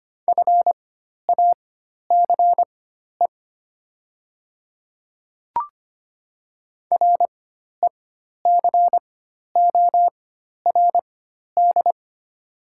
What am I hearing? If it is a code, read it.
Morse code, heard as a keyed tone, about 25 wpm, spaced as for 12 wpm, 700 Hz. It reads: FACE RECORD
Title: 4x Character Spacing - Top 500 Words - 25wpm